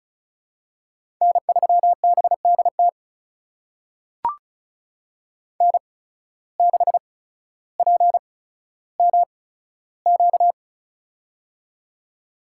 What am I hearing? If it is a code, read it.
Morse code, heard as a keyed tone, about 34 wpm, spaced as for 15 wpm, 700 Hz. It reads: N3BDT N6PMQ
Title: USA Call Signs - 5 Characters - 8x Character Spacing - 35wpm